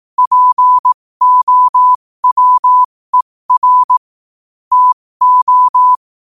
Morse code, heard as a keyed tone, about 18 wpm, spaced as for 15 wpm, 1000 Hz. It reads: POWER TO
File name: qotdfiles / 15wpm